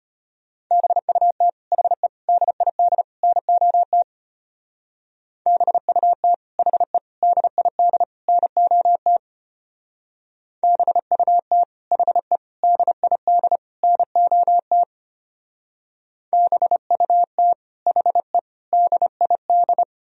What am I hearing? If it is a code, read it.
Morse code, heard as a keyed tone, about 33 wpm, 700 Hz. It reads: BUT HE DID NOT BUT HE DID NOT BUT HE DID NOT BUT HE DID